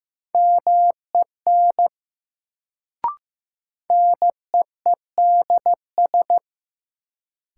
Morse code, heard as a keyed tone, about 15 wpm, 700 Hz. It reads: MEN NEEDS
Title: Sets of 3 Words - New Word Drill - 15wpm